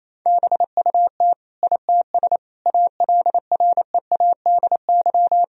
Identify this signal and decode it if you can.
Morse code, heard as a keyed tone, about 27 wpm, 700 Hz. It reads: BUT ITS ALREADY
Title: Sentences from Top 1000 Words - Review Part 2 - 28wpm